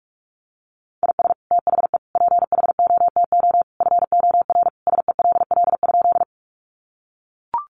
Morse code, heard as a keyed tone, about 44 wpm, 700 Hz. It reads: IS THE PHOTO FOR SELL?
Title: Sentences from Top 1200 Words 45wpm